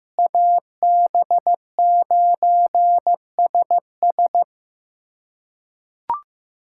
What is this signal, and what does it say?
Morse code, heard as a keyed tone, about 15 wpm, 700 Hz. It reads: AB9SS
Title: USA Callsign DE Callsign - 5 Characters - 15wpm